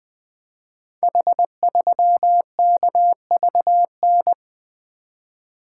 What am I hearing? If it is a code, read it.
Morse code, heard as a keyed tone, about 20 wpm, 700 Hz. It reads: H3KVN
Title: Code Groups - 5 Letter:Numbers - 20wpm